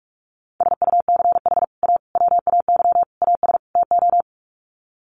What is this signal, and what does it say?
Morse code, heard as a keyed tone, about 44 wpm, 700 Hz. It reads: SUCH A WAY AS TO
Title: Sets of 5 Words Encore2 45wpm